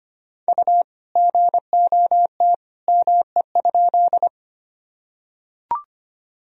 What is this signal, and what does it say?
Morse code, heard as a keyed tone, about 25 wpm, 700 Hz. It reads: U GOT ME?
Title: Sentences from Top 300 Words and Abbreviations 25wpm